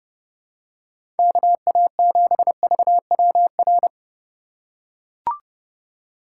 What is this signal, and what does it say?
Morse code, heard as a keyed tone, about 30 wpm, 700 Hz. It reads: KA7VWR
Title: USA Call Signs - 6 Characters - 6x Character Spacing - 30wpm